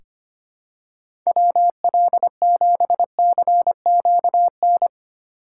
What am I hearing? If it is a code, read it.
Morse code, heard as a keyed tone, about 25 wpm, 700 Hz. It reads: WL7CQN
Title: USA Call Signs - 6 Characters - 4x Character Spacing - 25wpm